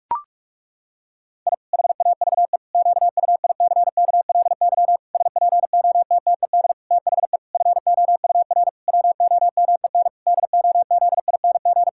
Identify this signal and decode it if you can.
Morse code, heard as a keyed tone, about 41 wpm, 700 Hz. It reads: I HAVE QUICKLY SPOTTED THE FOUR WOMEN DOZING
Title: Pangram Challenge v2 - 45wpm